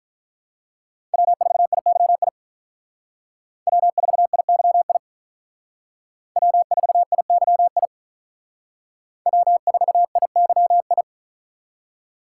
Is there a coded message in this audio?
Yes: Morse code from 1.1 to 11.0 s, heard as a keyed tone, about 45 wpm, 700 Hz. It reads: W4IYI W4IYI W4IYI W4IYI